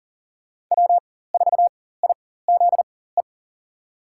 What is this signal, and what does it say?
Morse code, heard as a keyed tone, about 40 wpm, spaced as for 19 wpm, 700 Hz. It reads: W4IZE
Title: USA Call Signs - 5 Characters - 4x Character Spacing - 40wpm